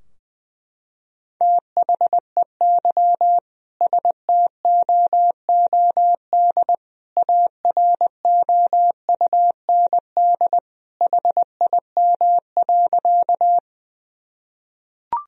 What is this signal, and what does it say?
Morse code, heard as a keyed tone, about 20 wpm, 700 Hz. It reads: THEY STOOD AROUND HIM.
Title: Sentences from Top 1000 Words - Review Part 3 - 20wpm